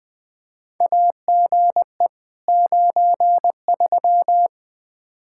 Morse code, heard as a keyed tone, about 20 wpm, 700 Hz. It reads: AGE 93
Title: Practice QSO - Exchange 3 - More Info - 20wpm